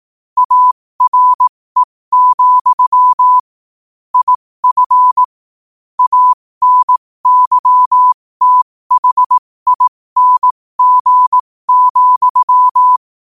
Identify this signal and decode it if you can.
Morse code, heard as a keyed tone, about 18 wpm, spaced as for 15 wpm, 1000 Hz. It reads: ARE, IF ANYTHING,
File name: qotdfiles / 15wpm